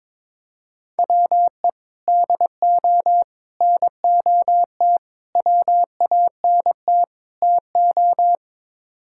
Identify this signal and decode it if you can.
Morse code, heard as a keyed tone, about 22 wpm, 700 Hz. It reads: WE DO NOT WANT TO
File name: practice / 22wpm / Build A Phrase - 5-Word Phrases - 22wpm